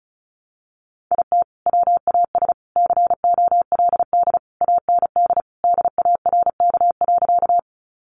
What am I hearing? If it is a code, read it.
Morse code, heard as a keyed tone, about 34 wpm, 700 Hz. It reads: IT WAS COLD AND DARK.
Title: Sentences from Top 500 Words 35wpm